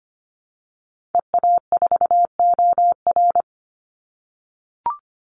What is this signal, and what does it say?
Morse code, heard as a keyed tone, about 25 wpm, 700 Hz. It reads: EA4OR
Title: Call Signs - 25wpm